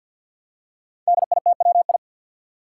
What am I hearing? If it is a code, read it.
Morse code, heard as a keyed tone, about 51 wpm, 700 Hz. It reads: DITWI
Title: Five Challenging Letter-Number Combinations 50wpm